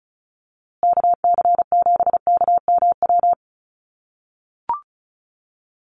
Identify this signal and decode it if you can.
Morse code, heard as a keyed tone, about 34 wpm, 700 Hz. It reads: KC7KMW
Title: USA Call Signs Repeated 3x 35wpm